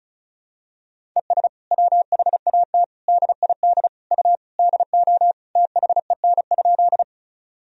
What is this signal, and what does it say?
Morse code, heard as a keyed tone, about 36 wpm, 700 Hz. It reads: ES WHAT DID U DO THEN?
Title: Sentences from Top 100 Words and Abbreviations 35wpm